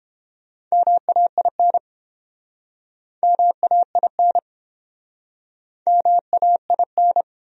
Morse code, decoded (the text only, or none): MAIN MAIN MAIN